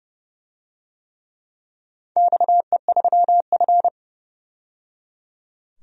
Morse code, heard as a keyed tone, about 31 wpm, 700 Hz. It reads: XE3F
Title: DXCC-100 Call Signs - 4 Characters - 30wpm